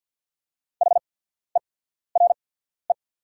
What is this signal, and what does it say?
Morse code, heard as a keyed tone, about 48 wpm, spaced as for 14 wpm, 690 Hz. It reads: HERE
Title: 8x Character Spacing - Top 200 Words - 50wpm